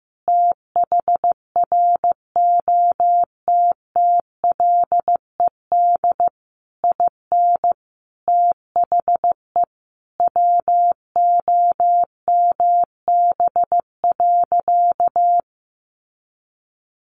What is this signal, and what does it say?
Morse code, heard as a keyed tone, about 15 wpm, 700 Hz. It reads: THROTTLED IN THE WOMB.
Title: A Sound Of Thunder 15wpm